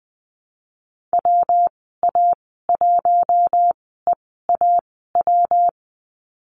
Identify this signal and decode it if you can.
Morse code, heard as a keyed tone, about 20 wpm, spaced as for 15 wpm, 700 Hz. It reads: WA1EAW
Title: 231 Lesson 37g - Letter x - Call Signs 20wpm